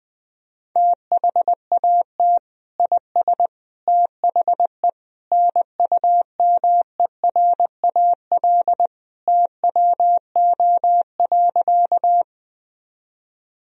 Morse code, heard as a keyed tone, about 20 wpm, 700 Hz. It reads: THAT IS THE NUMERAL TWO.
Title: Sentences from Top 1000 Words - Review Part 2 - 20wpm